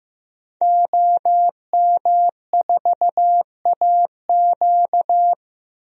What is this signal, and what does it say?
Morse code, heard as a keyed tone, about 15 wpm, 700 Hz. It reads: OM4AQ
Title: DXCC-100 Call Signs - 5 Characters - Repeated 3x - 15wpm